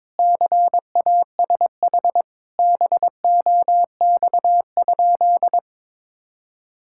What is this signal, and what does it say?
Morse code, heard as a keyed tone, about 21 wpm, 700 Hz. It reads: CASH BOX?
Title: Sentences from Top 1000 Words - Review Part 5 - 22wpm